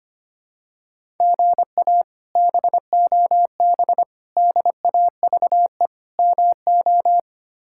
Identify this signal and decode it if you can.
Morse code, heard as a keyed tone, about 25 wpm, 700 Hz. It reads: GA BOB DAVE MO
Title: K1USN SST QSO Practice - 25wpm